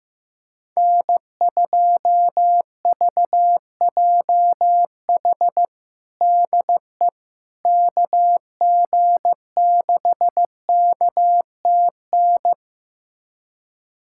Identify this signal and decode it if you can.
Morse code, heard as a keyed tone, about 15 wpm, 700 Hz. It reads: N2VJH DE KG6KTN
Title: USA Callsign DE Callsign 15wpm